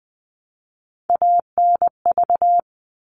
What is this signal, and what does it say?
Morse code, heard as a keyed tone, about 20 wpm, 700 Hz. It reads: ANV
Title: All Cut Numbers - 3-Digit 20wpm